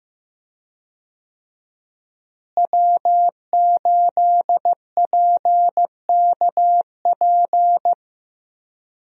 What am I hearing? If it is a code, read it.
Morse code, heard as a keyed tone, about 15 wpm, 700 Hz. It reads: W8PKP
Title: USA Call Signs - 5 Characters - Repeated 2x - 15wpm